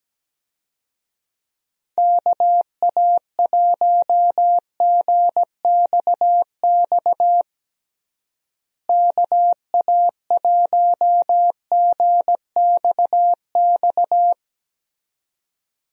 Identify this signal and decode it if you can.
Morse code, heard as a keyed tone, about 17 wpm, 700 Hz. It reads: KA1GXX KA1GXX
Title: USA Call Signs - 6 Characters - Repeated 2x - 17wpm